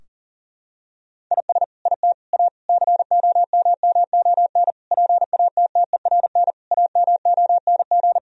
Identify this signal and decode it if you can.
Morse code, heard as a keyed tone, about 40 wpm, 700 Hz. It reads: IS IT A COMMON PATTERN AMONG
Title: Sentences from Top 1000 Words - Review Part 2 - 40wpm